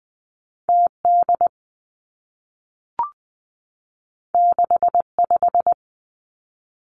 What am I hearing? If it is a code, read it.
Morse code, heard as a keyed tone, about 20 wpm, 700 Hz. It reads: T D 6 5
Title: Two Challenging Letter-Number Combinations 20wpm